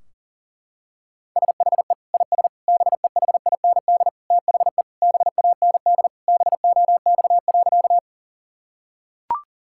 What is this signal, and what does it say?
Morse code, heard as a keyed tone, about 40 wpm, 700 Hz. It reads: SHE IS BEHIND THE BAND BOX.
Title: Sentences from Top 1600 Words 40wpm